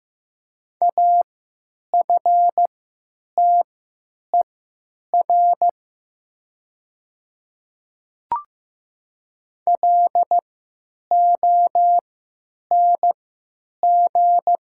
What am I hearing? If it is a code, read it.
Morse code, heard as a keyed tone, about 15 wpm, spaced as for 9 wpm, 700 Hz. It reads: AFTER LONG
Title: 3x Character Spacing - Top 200 Words - 15wpm